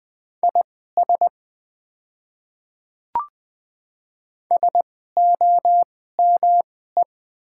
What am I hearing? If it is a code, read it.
Morse code, heard as a keyed tone, about 20 wpm, spaced as for 15 wpm, 700 Hz. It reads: IS SOME
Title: 2x Character Spacing - Top 100 Words - 20wpm